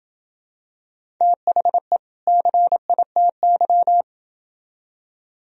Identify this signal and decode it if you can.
Morse code, heard as a keyed tone, about 26 wpm, 700 Hz. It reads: THE CITY